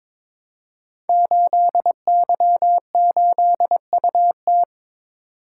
Morse code, heard as a keyed tone, about 22 wpm, 700 Hz. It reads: 8Y8UT
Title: Code Groups - 5 Letter:Numbers - 22wpm